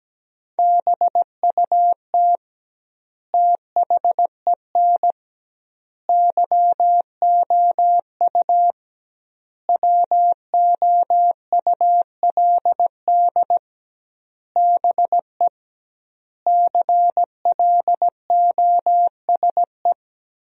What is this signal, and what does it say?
Morse code, heard as a keyed tone, about 17 wpm, 700 Hz. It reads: BUT THEN YOU WOULD BE CLOSE
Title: 2x Word-Spacing - Sentences from Top 300 Words - 17wpm